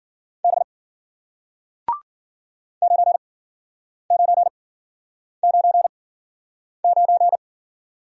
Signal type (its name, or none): Morse code